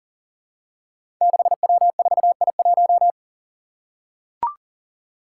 Morse code, heard as a keyed tone, about 40 wpm, 700 Hz. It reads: 6W4I1
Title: Five Challenging Letter-Number Combinations 40wpm